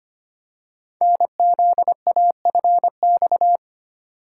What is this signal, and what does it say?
Morse code, heard as a keyed tone, about 25 wpm, 700 Hz. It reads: NZAFX
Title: Code Groups - 5 Letters - 25wpm